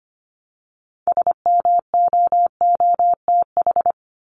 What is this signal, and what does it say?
Morse code, heard as a keyed tone, about 25 wpm, 700 Hz. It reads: SMOOTH
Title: Top 1251-1500 Words 25wpm